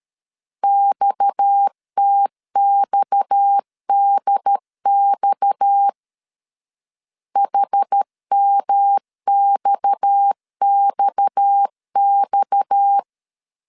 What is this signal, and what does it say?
Morse code, heard as a keyed tone, about 13 wpm, 800 Hz. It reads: XTXDX HMXXX
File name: LEZIONI / 6 PKX0 / 6 PKX 12wpm 2024